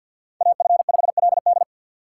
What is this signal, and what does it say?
Morse code, heard as a keyed tone, about 52 wpm, 700 Hz. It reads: AF5LD
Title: USA Call Signs - 5 Characters - 2x Character Spacing - 50wpm